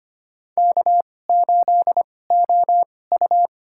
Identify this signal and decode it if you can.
Morse code, heard as a keyed tone, about 25 wpm, spaced as for 18 wpm, 700 Hz. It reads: K8OU
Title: USA Call Signs - 4 Characters - 2x Character Spacing - 25wpm